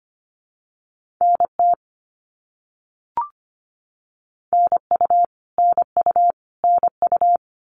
Morse code, heard as a keyed tone, about 25 wpm, 700 Hz. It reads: NT NU NU NU
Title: Canadian Provinces and Territories - Repeated 3x - 25wpm